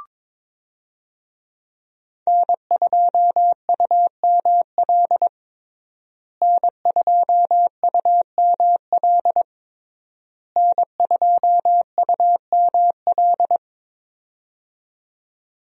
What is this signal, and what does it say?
Morse code, heard as a keyed tone, about 21 wpm, 700 Hz. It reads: N2UML N2UML N2UML
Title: USA Call Signs - 5 Characters - Repeated 3x - 22wpm